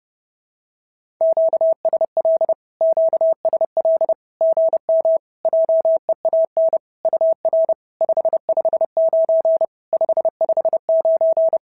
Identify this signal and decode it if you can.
Morse code, heard as a keyed tone, about 30 wpm, 650 Hz. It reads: QSL QSL GM JEAN UR 559 559
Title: SOTA QSO Practice 30wpm